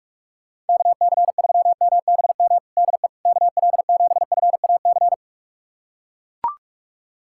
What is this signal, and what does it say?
Morse code, heard as a keyed tone, about 41 wpm, 700 Hz. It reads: KC3MBM DE KL7FAC
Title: USA Callsign DE Callsign 45wpm